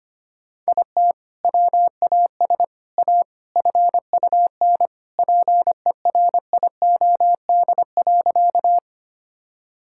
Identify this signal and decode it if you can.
Morse code, heard as a keyed tone, about 25 wpm, 700 Hz. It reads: IT WAS A FUN PERIOD.
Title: Sentences from Top 1000 Words - Review Part 5 - 25wpm